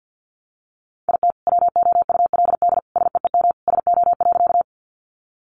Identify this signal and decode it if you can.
Morse code, heard as a keyed tone, about 54 wpm, 700 Hz. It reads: IT WOULD SEEM SO.
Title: Sentences from Top 300 Words 50wpm